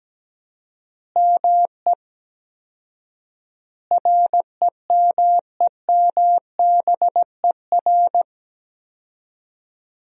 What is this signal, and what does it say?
Morse code, heard as a keyed tone, about 17 wpm, 700 Hz. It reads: ME REMEMBER
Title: 4x Word-Spacing - Sentences from Top 400 Words - 17wpm